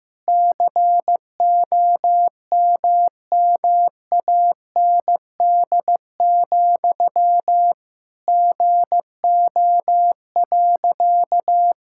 Morse code, heard as a keyed tone, about 15 wpm, 700 Hz. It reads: COMMAND, GO.
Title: Sentences from Top 1600 Words 15wpm